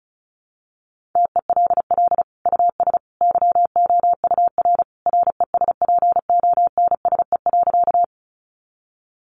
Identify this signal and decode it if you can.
Morse code, heard as a keyed tone, about 34 wpm, 700 Hz. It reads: TELL US YOUR RESPONSE.
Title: Sentences from Top 1400 Words 35wpm